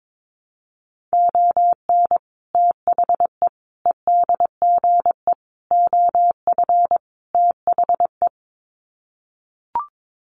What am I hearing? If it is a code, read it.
Morse code, heard as a keyed tone, about 22 wpm, 700 Hz. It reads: ON THE EDGE OF THE
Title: Sets of 5 Words 22wpm